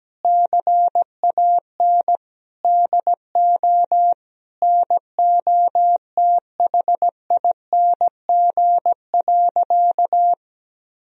Morse code, heard as a keyed tone, about 17 wpm, 700 Hz. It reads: CAN DO NOTHING.
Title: Sentences from Top 1000 Words - Review Part 3 - 17wpm